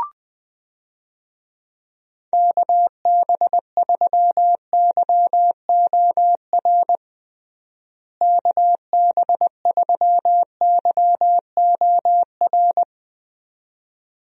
Morse code, heard as a keyed tone, about 20 wpm, 700 Hz. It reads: KB3YOR KB3YOR
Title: USA Call Signs - 6 Characters - Repeated 2x - 20wpm